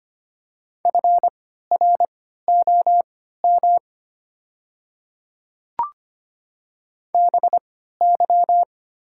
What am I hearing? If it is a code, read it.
Morse code, heard as a keyed tone, about 25 wpm, spaced as for 14 wpm, 700 Hz. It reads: FROM BY
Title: 3x Character Spacing - Top 100 Words - 25wpm